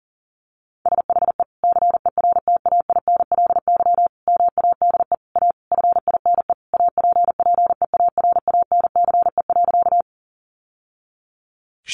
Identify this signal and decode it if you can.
Morse code, heard as a keyed tone, about 40 wpm, 700 Hz. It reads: SHE CERTAINLY MADE A FINE APPEARANCE.
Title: Sentences from Top 1100 Words 40wpm